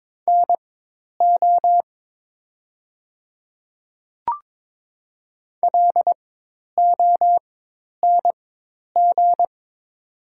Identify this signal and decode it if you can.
Morse code, heard as a keyed tone, about 23 wpm, spaced as for 10 wpm, 700 Hz. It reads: NO LONG
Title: 4x Character Spacing - Top 100 Words - 22wpm